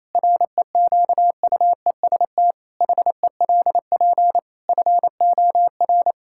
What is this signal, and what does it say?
Morse code, heard as a keyed tone, about 28 wpm, 700 Hz. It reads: REQUEST HELP FOR
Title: Pangram Challenge v2 - 28wpm